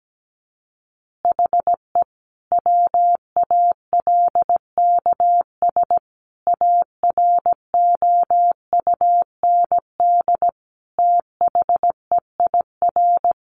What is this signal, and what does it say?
Morse code, heard as a keyed tone, about 17 wpm, 700 Hz. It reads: HE WALKS AROUND THEIR
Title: Easy Phrases 17wpm